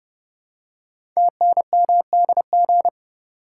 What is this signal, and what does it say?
Morse code, heard as a keyed tone, about 30 wpm, 700 Hz. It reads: TNMDG
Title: Code Groups - 5 Letters - 30wpm